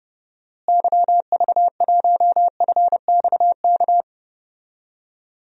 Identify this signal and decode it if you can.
Morse code, heard as a keyed tone, about 30 wpm, 700 Hz. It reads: YV1FXK
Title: DXCC-100 Call Signs - 6 Characters - Repeated 3x - 30wpm